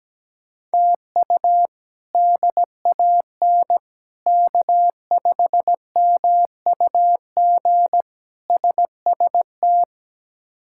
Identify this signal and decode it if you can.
Morse code, heard as a keyed tone, about 17 wpm, 700 Hz. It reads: TU DAN K5MUG SST
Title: K1USN SST QSO Practice - 17wpm